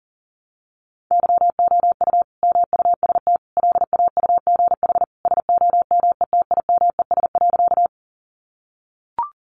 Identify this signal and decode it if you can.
Morse code, heard as a keyed tone, about 37 wpm, 700 Hz. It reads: YOU MUST LAUGH SOMETIMES.
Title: Sentences from Top 600 Words 40wpm